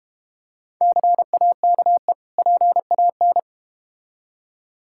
Morse code, heard as a keyed tone, about 30 wpm, 700 Hz. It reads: CAKE PAN
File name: Easy Words - Speed-Racing with Hints - 25wpm